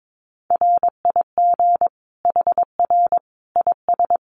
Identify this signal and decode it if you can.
Morse code, heard as a keyed tone, about 22 wpm, 700 Hz. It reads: RIG HR IS
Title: Common QSO Phrases 22wpm